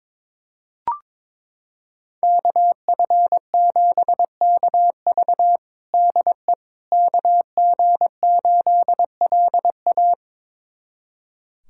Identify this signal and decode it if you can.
Morse code, heard as a keyed tone, about 22 wpm, 700 Hz. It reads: KF7KV DE KG8LA
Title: USA Callsign DE Callsign - 5 Characters - 22wpm